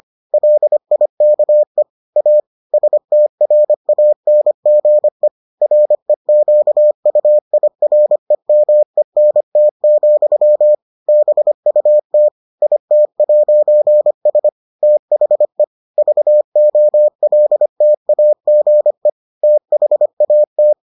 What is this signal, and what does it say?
Morse code, heard as a keyed tone, about 25 wpm, 600 Hz. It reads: LIKE A STRANGE REQUIREMENT, BUT IT'S THE VOLTAGE THAT